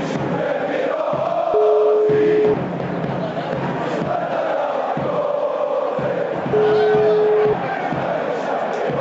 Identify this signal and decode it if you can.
Ringback tone (Europe, 425 Hz), with noise.